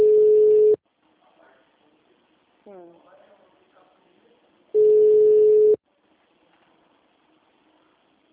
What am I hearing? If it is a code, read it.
Ringback tone (Europe, 425 Hz), with speech.